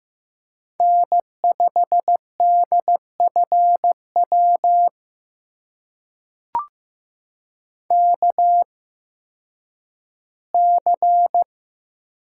Morse code, heard as a keyed tone, about 15 wpm, spaced as for 10 wpm, 700 Hz. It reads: N5DFW KC
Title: USA Call Signs - 8x Character Spacing - 15wpm